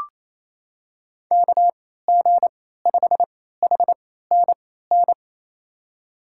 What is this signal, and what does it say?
Morse code, heard as a keyed tone, about 27 wpm, spaced as for 16 wpm, 700 Hz. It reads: KG5HNN
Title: USA Call Signs - 6 Characters - 3x Character Spacing - 28wpm